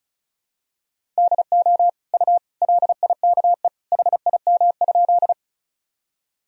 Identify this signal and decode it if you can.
Morse code, heard as a keyed tone, about 36 wpm, 700 Hz. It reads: DO U LIKE HIM?
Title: Sentences from Top 100 Words and Abbreviations 35wpm